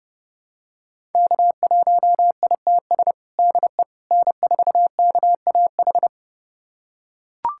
Morse code, heard as a keyed tone, about 30 wpm, 700 Hz. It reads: K1ITS DE N4KAH
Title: USA Callsign DE Callsign - 5 Characters - 30wpm